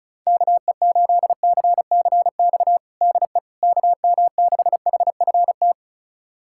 Morse code, heard as a keyed tone, about 36 wpm, 700 Hz. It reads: KE8CCX DE KM6HFT
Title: USA Callsign DE Callsign - 6 Characters - 35wpm